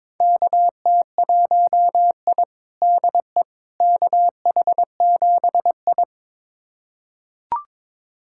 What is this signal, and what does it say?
Morse code, heard as a keyed tone, about 22 wpm, 700 Hz. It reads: KT1I DE KH7I
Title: USA Callsign DE Callsign - 4 Characters - 22wpm